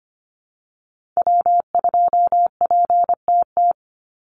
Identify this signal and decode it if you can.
Morse code, heard as a keyed tone, about 25 wpm, 700 Hz. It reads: W2PTT
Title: Call Signs Repeated 3x - 25wpm